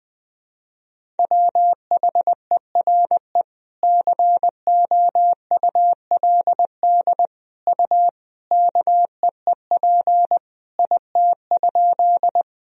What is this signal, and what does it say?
Morse code, heard as a keyed tone, about 20 wpm, 700 Hz. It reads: WHERE COULD U KEEP IT?
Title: Sentences from Top 300 Words and Abbreviations 20wpm